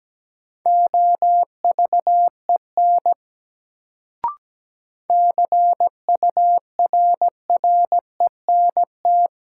Morse code, heard as a keyed tone, about 17 wpm, 700 Hz. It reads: OVEN CURRENT
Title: Sentences from Top 1000 Words - Review Part 4 - New Word Drill - 17wpm